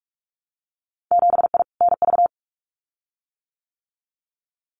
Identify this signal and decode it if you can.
Morse code, heard as a keyed tone, about 46 wpm, 700 Hz. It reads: 7I NV
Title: Winter Field Day Exchanges Encore 45wpm